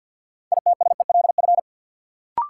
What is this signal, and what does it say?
Morse code, heard as a keyed tone, about 50 wpm, 700 Hz. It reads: ITSELF
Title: Sentences from Top 800 Words - New Word Drill - 50wpm